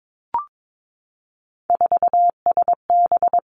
Morse code, heard as a keyed tone, about 22 wpm, 700 Hz. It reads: 4SB
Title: Three Challenging Letter-Number Combinations 22wpm